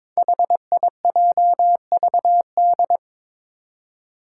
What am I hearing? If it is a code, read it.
Morse code, heard as a keyed tone, about 22 wpm, 700 Hz. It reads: HIJVD
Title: Five Challenging Letter-Number Combinations 22wpm